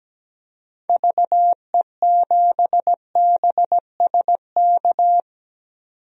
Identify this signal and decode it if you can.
Morse code, heard as a keyed tone, about 17 wpm, 700 Hz. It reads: VE7BSK
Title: DXCC-100 Call Signs - 6 Characters - 17wpm